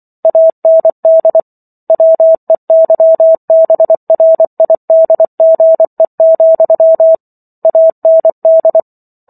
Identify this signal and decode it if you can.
Morse code, heard as a keyed tone, about 24 wpm, 650 Hz. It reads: AND WEYBRIDGE, AND